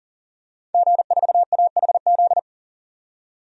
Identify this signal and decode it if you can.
Morse code, heard as a keyed tone, about 41 wpm, 700 Hz. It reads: G4AHZ
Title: DXCC-100 Call Signs - 5 Characters - 40wpm